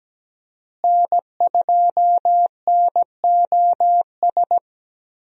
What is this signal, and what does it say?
Morse code, heard as a keyed tone, about 17 wpm, 700 Hz. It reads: N2NOS
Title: USA Call Signs - 5 Characters - Repeated 3x - 17wpm